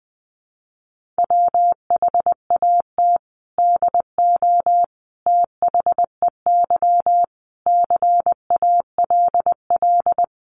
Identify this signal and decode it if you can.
Morse code, heard as a keyed tone, about 20 wpm, 700 Hz. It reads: WHAT DO THEY CALL
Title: Sentences from Top 100 Words 20wpm